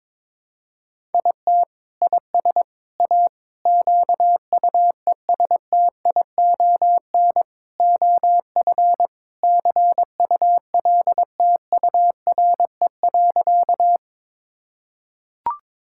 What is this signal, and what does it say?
Morse code, heard as a keyed tone, about 22 wpm, 700 Hz. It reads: IT IS A QUESTION OF CULTURE.
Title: Sentences from Top 1000 Words - Review Part 6 - 22wpm